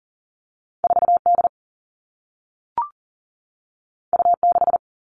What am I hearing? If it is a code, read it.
Morse code, heard as a keyed tone, about 39 wpm, 700 Hz. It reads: 4 D U 6
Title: Two Challenging Letter-Number Combinations 40wpm